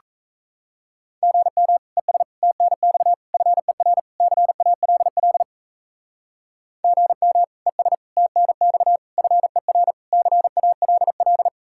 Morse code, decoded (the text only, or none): GM ES TNX FER CALL GM ES TNX FER CALL